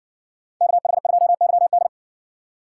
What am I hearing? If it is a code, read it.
Morse code, heard as a keyed tone, about 52 wpm, 700 Hz. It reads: KH2YD